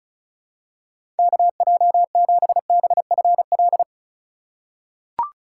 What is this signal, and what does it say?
Morse code, heard as a keyed tone, about 36 wpm, 700 Hz. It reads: KJ7BFL
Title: USA Call Signs - 6 Characters - 3x Character Spacing - 35wpm